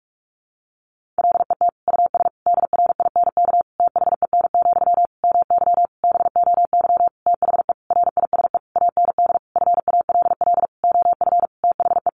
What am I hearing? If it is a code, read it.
Morse code, heard as a keyed tone, about 44 wpm, 700 Hz. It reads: LET US DRINK THEN, MY BOY THE RISE AND FALL OF THE
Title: Sentences from Top 1500 Words 45wpm